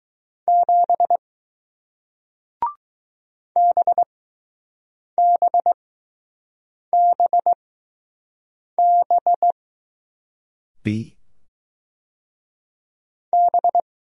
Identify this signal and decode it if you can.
Morse code, heard as a keyed tone, about 20 wpm, spaced as for 6 wpm, 700 Hz. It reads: 7 BBBB B